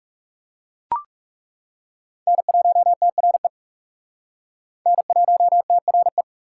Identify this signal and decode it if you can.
Morse code, heard as a keyed tone, about 40 wpm, 700 Hz. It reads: N1TRE N1TRE